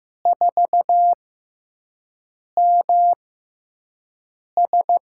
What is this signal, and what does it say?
Morse code, heard as a keyed tone, about 15 wpm, spaced as for 5 wpm, 700 Hz. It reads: 4MS
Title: USA Call Signs - 6x Character Spacing - 15wpm